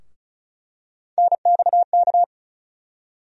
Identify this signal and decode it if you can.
Morse code, heard as a keyed tone, about 34 wpm, 700 Hz. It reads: NXK
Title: USA Call Sign Postfixes - 3 Letters 35wpm